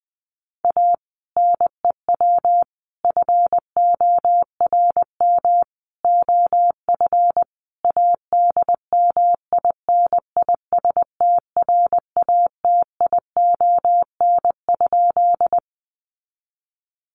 Morse code, heard as a keyed tone, about 20 wpm, 700 Hz. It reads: A NEW FORM OF ADMINISTRATION?
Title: Sentences from Top 1300 Words 20wpm